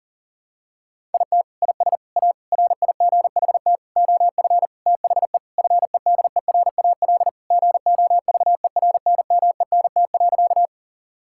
Morse code, heard as a keyed tone, about 40 wpm, 700 Hz. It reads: IT IS A RIGHT OF THE FEDERAL GOVERNMENT.
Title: Sentences from Top 1000 Words - Review Part 5 - 40wpm